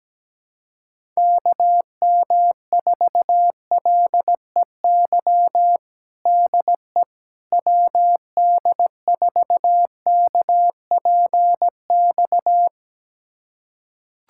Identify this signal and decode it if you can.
Morse code, heard as a keyed tone, about 17 wpm, 700 Hz. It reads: KM4LEY DE WD4KPX
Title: USA Callsign DE Callsign 17wpm